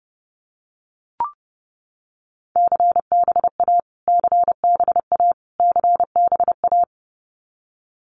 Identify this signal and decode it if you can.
Morse code, heard as a keyed tone, about 30 wpm, 700 Hz. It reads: CBA CBA CBA
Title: Abbreviations Repeated 3x - 30wpm